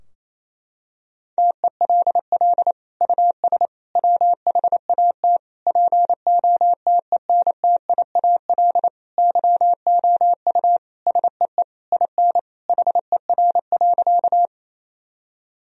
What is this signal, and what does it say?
Morse code, heard as a keyed tone, about 28 wpm, 700 Hz. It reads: TELL US WHAT POTENTIAL YOU SEE IN HER.
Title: Sentences from Top 1000 Words - Review Part 4 - 28wpm